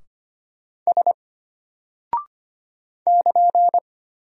Morse code, heard as a keyed tone, about 25 wpm, 700 Hz. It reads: S <KN>
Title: Single Character (Letter:Number:Punctuation:Prosign) Alt 25wpm